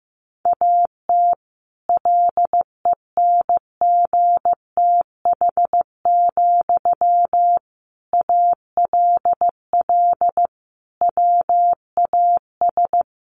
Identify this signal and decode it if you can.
Morse code, heard as a keyed tone, about 15 wpm, 700 Hz. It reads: AT LENGTH, ALL WAS
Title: Sentences from Top 1000 Words 15wpm